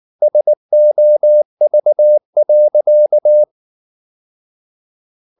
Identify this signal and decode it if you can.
Morse code, heard as a keyed tone, about 19 wpm, 600 Hz. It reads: SOV.